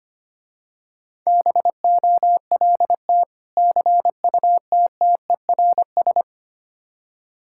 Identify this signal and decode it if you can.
Morse code, heard as a keyed tone, about 25 wpm, 700 Hz. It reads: BOLT CUTTERS
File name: practice / 25wpm / Difficult Words with Hints - 25wpm